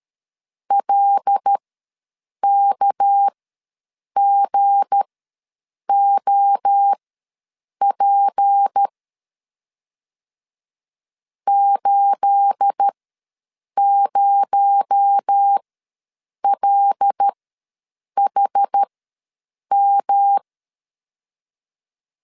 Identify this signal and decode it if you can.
Morse code, heard as a keyed tone, about 13 wpm, spaced as for 7 wpm, 800 Hz. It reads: LKGOP 80LHM
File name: LEZIONI / 9 28 ws 12wpm 3354